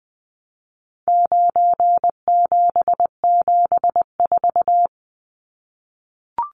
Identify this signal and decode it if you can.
Morse code, heard as a keyed tone, about 20 wpm, 700 Hz. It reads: 9774
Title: Numbers - 4-Digit 20wpm